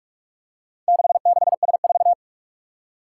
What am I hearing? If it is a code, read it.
Morse code, heard as a keyed tone, about 44 wpm, 700 Hz. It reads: 66S4
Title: Four Challenging Letter-Number Combinations 45wpm